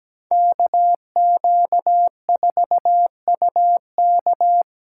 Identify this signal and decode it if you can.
Morse code, heard as a keyed tone, about 20 wpm, 700 Hz. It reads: KQ4UK
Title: Speed Racing - USA 2x2 Call Signs 17wpm